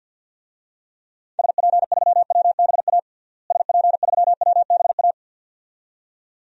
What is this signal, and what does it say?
Morse code, heard as a keyed tone, about 48 wpm, 700 Hz. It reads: SP3WBA SP3WBA
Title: DXCC-100 Call Signs - 6 Characters - Repeated 2x - 50wpm